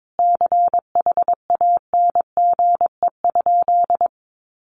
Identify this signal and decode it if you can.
Morse code, heard as a keyed tone, about 22 wpm, 700 Hz. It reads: CHANGE?
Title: Sentences from Top 1500 Words 22wpm